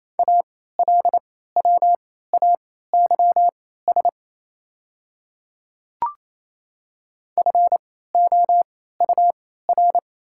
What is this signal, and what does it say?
Morse code, heard as a keyed tone, about 28 wpm, spaced as for 16 wpm, 700 Hz. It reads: ALWAYS FOUR
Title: 3x Character Spacing - Top 300 Words - 28wpm